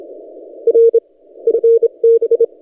Morse code, heard as a keyed tone, about 27 wpm, 460 Hz. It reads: R FB
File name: posts / t32rc-30cw